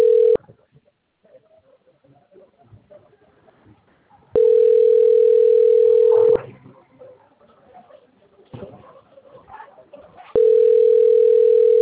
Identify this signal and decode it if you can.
Ringback tone (North America), 440 + 480 Hz.
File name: rg-1600-unknown- - - .40871